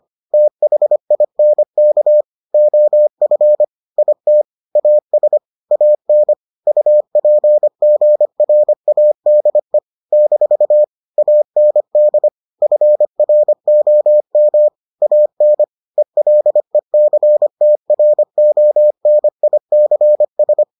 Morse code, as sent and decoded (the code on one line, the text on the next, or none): - .... .. -. -.- / --- ..-. / .. - / .- ... / .- -. / ..- .--. --. .-. .- -.. / -....- / .- -. -.. / ..-. .-. --- -- / .- -. / . .-.. -.-. - .-. --- -. .. -.-. ...
THINK OF IT AS AN UPGRADE - AND FROM AN ELECTRONICS